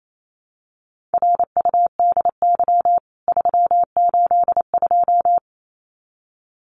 Morse code, heard as a keyed tone, about 28 wpm, 700 Hz. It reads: RUDY 382
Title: CWT Exchange 28wpm